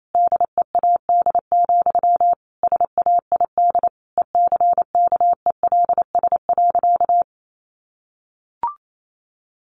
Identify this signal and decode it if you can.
Morse code, heard as a keyed tone, about 28 wpm, 700 Hz. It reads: DEAD, SAID ECKELS.
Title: A Sound Of Thunder 28wpm